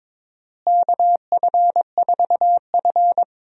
Morse code, heard as a keyed tone, about 22 wpm, 700 Hz. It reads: KF4F
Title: USA Call Signs - 4 Characters - Repeated 3x - 22wpm